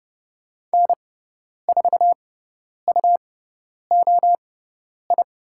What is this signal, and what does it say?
Morse code, heard as a keyed tone, about 30 wpm, spaced as for 10 wpm, 700 Hz. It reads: N4UOI
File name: Lesson 17g - Letter u - Call Signs 30wpm